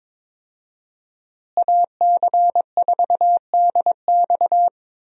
Morse code, heard as a keyed tone, about 22 wpm, 700 Hz. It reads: AC4DX
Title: USA Call Signs - 5 Characters - Repeated 3x - 22wpm